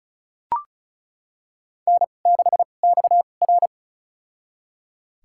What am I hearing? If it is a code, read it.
Morse code, heard as a keyed tone, about 34 wpm, spaced as for 25 wpm, 700 Hz. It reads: N6XR
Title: USA Call Signs - 4 Characters - 2x Character Spacing - 35wpm